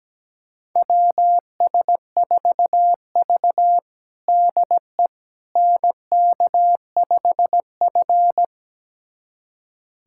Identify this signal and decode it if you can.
Morse code, heard as a keyed tone, about 17 wpm, 700 Hz. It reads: WS4V DE NK5F
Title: USA Callsign DE Callsign - 4 Characters - 17wpm